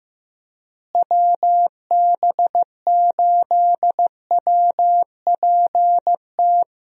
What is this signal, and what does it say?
Morse code, heard as a keyed tone, about 15 wpm, 700 Hz. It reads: WB8WPT
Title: USA Call Signs - 6 Characters - Repeated 3x - 15wpm